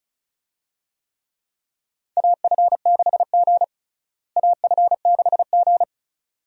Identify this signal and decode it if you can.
Morse code, heard as a keyed tone, about 36 wpm, 700 Hz. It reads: AF6G AF6G
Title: USA Call Signs - 4 Characters - Repeated 2x - 35wpm